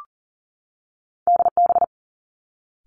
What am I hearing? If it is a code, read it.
Morse code, heard as a keyed tone, about 42 wpm, 700 Hz. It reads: D B
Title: Two Challenging Letter-Number Combinations 40wpm